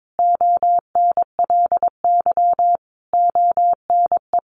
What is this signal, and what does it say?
Morse code, heard as a keyed tone, about 22 wpm, 700 Hz. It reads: ONLY ONE
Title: Sentences from Top 1200 Words 22wpm